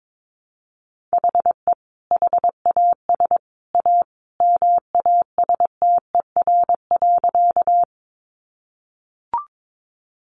Morse code, heard as a keyed tone, about 22 wpm, 700 Hz. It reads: HE HAS A MASTER.
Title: Sentences from Top 1300 Words 22wpm